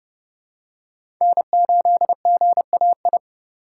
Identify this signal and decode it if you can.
Morse code, heard as a keyed tone, about 31 wpm, 700 Hz. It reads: N8GAI
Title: USA Call Signs - 2x Character Spacing - 30wpm